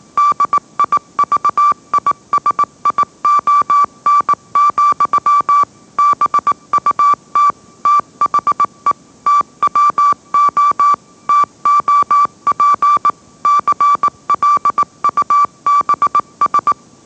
Morse code, heard as a keyed tone, about 18 wpm, 1160 Hz. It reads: DIVISION, BUT THE TWO TOP CLUBS
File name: mrx / mrx081 20wpm